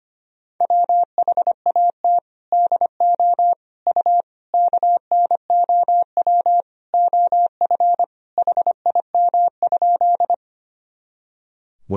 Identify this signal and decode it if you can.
Morse code, heard as a keyed tone, about 25 wpm, 700 Hz. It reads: WHAT DO U KNOW OF HIM?
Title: Sentences from Top 100 Words and Abbreviations 25wpm